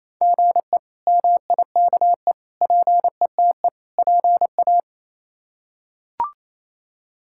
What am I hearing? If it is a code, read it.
Morse code, heard as a keyed tone, about 28 wpm, 700 Hz. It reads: GE MIKE PETE PA
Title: K1USN SST QSO Practice - 28wpm